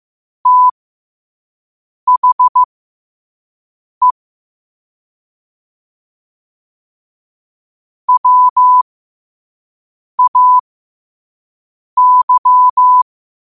Morse code, heard as a keyed tone, about 15 wpm, spaced as for 5 wpm, 1000 Hz. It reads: THE WAY
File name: qotdfiles / 5wpm